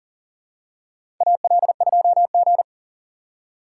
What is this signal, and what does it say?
Morse code, heard as a keyed tone, about 40 wpm, 700 Hz. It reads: AL2G
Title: USA Call Signs - 4 Characters - 6x Character Spacing - 40wpm